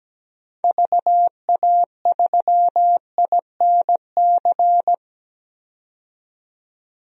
Morse code, heard as a keyed tone, about 17 wpm, 700 Hz. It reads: VA3INC
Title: DXCC-100 Call Signs - 6 Characters - 17wpm